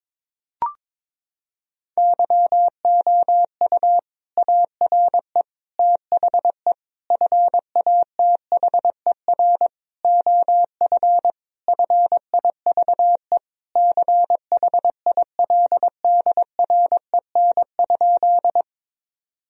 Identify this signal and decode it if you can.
Morse code, heard as a keyed tone, about 22 wpm, 700 Hz. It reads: YOU ARE THE FATHER OF FIVE CHILDREN?
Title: Sentences from Top 1000 Words - Review Part 2 - 22wpm